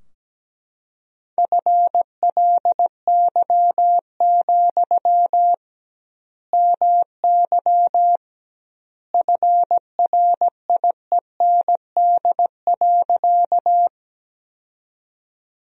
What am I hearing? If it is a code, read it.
Morse code, heard as a keyed tone, about 17 wpm, 700 Hz. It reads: FLY, MY FRIEND.
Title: 2x Word-Spacing - Sentences from Top 500 Words - 17wpm